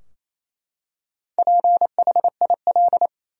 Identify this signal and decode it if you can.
Morse code, heard as a keyed tone, about 28 wpm, 700 Hz. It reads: PHIL
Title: USA Common Names - 28wpm